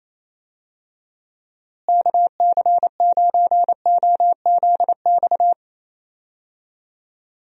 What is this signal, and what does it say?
Morse code, heard as a keyed tone, about 28 wpm, 700 Hz. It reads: KC9OZX
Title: USA Call Signs - 6 Characters - Repeated 2x - 28wpm